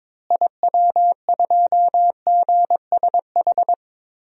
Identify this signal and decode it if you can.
Morse code, heard as a keyed tone, about 22 wpm, 700 Hz. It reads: IW2GSH
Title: DXCC-100 Call Signs - 6 Characters - Repeated 3x - 22wpm